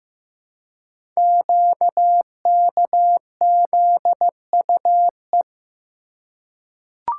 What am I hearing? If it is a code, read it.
Morse code, heard as a keyed tone, about 15 wpm, 700 Hz. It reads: QKZUE
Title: Code Groups - 5 Letters - 15wpm